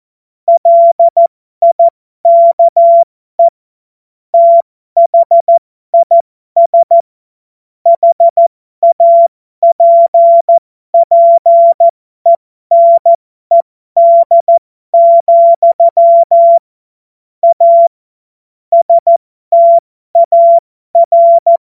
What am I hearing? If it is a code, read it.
Morse code, heard as a keyed tone, about 14 wpm, spaced as for 12 wpm, 680 Hz. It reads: LIKE THIS HAPPENED, A STAR